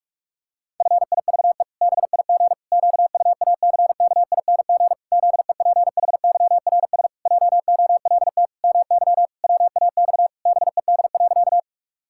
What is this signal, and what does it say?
Morse code, heard as a keyed tone, about 46 wpm, 700 Hz. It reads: FIVE BIG QUACKING ZEPHYRS JOLT MY WAX BED.
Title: Pangram Challenge v2 - 45wpm